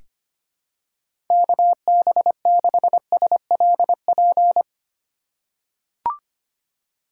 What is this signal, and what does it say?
Morse code, heard as a keyed tone, about 25 wpm, 700 Hz. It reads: KB6SLP
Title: USA Call Signs - 6 Characters - 2x Character Spacing - 25wpm